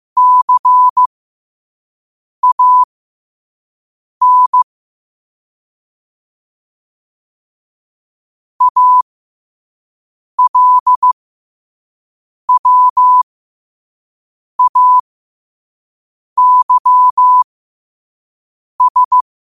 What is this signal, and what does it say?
Morse code, heard as a keyed tone, about 15 wpm, spaced as for 5 wpm, 1000 Hz. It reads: CAN ALWAYS
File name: qotdfiles / 5wpm